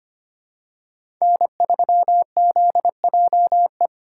Morse code, heard as a keyed tone, about 25 wpm, 700 Hz. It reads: N3ZJE
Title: USA Call Signs - 5 Characters - 2x Character Spacing - 25wpm